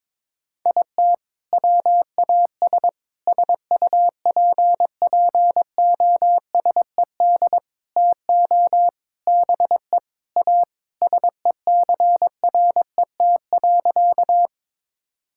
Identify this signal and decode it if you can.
Morse code, heard as a keyed tone, about 22 wpm, 700 Hz. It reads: IT WAS SUPPOSED TO BE A SECRET.
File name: practice / 22wpm / Sentences with 4-Word Phrases - 22wpm